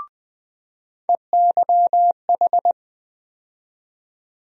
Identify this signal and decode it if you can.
Morse code, heard as a keyed tone, about 20 wpm, 700 Hz. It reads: EYH
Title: USA Call Sign Postfixes - 3 Letters 20wpm